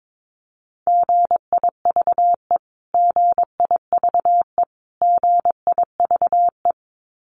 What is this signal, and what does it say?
Morse code, heard as a keyed tone, about 22 wpm, 700 Hz. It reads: GIVE GIVE GIVE
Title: Top 500 Words Repeated 22wpm